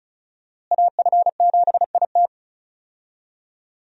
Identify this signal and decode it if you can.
Morse code, heard as a keyed tone, about 36 wpm, 700 Hz. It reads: AF7IT
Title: USA Call Signs - 5 Characters - Repeated 3x - 35wpm